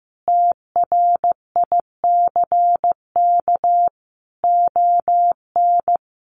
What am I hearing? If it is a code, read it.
Morse code, heard as a keyed tone, about 15 wpm, 700 Hz. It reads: TRICK ON
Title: Sentences from Top 1500 Words 15wpm